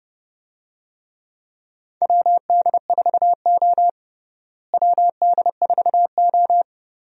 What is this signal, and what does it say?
Morse code, heard as a keyed tone, about 31 wpm, 700 Hz. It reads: WD4O WD4O
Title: USA Call Signs - 4 Characters - Repeated 3x - 30wpm